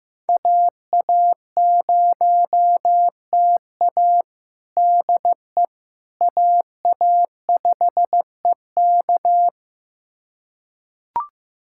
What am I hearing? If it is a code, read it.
Morse code, heard as a keyed tone, about 15 wpm, 700 Hz. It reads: AA0TA DE AA5EK
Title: USA Callsign DE Callsign - 5 Characters - 15wpm